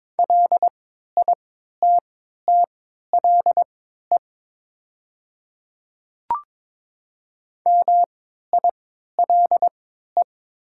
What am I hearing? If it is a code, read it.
Morse code, heard as a keyed tone, about 21 wpm, spaced as for 12 wpm, 700 Hz. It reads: LITTLE MILE
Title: 3x Character Spacing - Top 300 Words - 22wpm